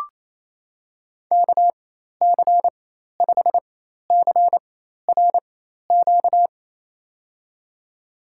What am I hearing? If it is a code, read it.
Morse code, heard as a keyed tone, about 27 wpm, spaced as for 13 wpm, 700 Hz. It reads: KC5CRQ
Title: USA Call Signs - 6 Characters - 4x Character Spacing - 28wpm